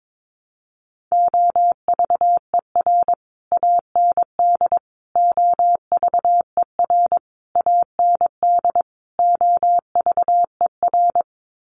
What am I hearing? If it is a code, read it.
Morse code, heard as a keyed tone, about 22 wpm, 700 Hz. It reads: OVER AND OVER AND OVER
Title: Sets of 5 Words Encore 22wpm